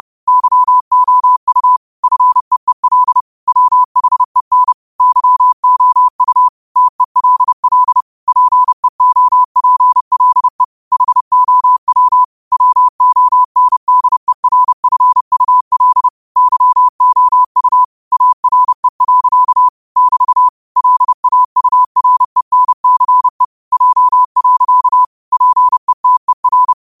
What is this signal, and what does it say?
Morse code, heard as a keyed tone, about 30 wpm, 1000 Hz. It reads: YOU FEEL WHEN YOU TELL PEOPLE HOW WONDERFUL YOU ARE. <BT> LAURENCE J. PETER